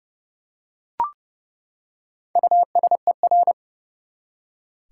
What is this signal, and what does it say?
Morse code, heard as a keyed tone, about 31 wpm, 700 Hz. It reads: USER
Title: Four Letter Words - 30wpm